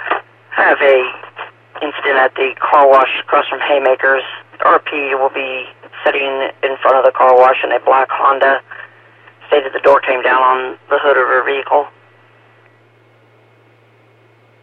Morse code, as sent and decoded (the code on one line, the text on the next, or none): none